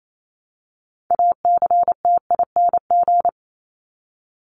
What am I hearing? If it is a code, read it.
Morse code, heard as a keyed tone, about 28 wpm, 700 Hz. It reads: ACTING
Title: ING Suffix - Top 100 Words with Base Words 28wpm